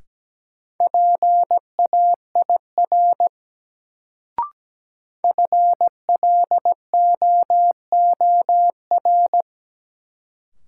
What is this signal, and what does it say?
Morse code, heard as a keyed tone, about 17 wpm, 700 Hz. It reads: PAIR FLOOR
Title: Sets of 3 Words - New Word Drill - 17wpm